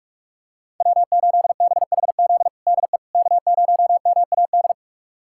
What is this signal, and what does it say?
Morse code, heard as a keyed tone, about 46 wpm, 700 Hz. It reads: W8BHZ DE K0MAD
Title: USA Callsign DE Callsign - 5 Characters - 45wpm